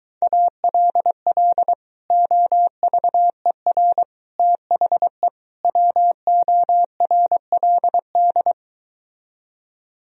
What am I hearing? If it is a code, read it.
Morse code, heard as a keyed tone, about 24 wpm, 700 Hz. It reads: ALL OVER THE WORLD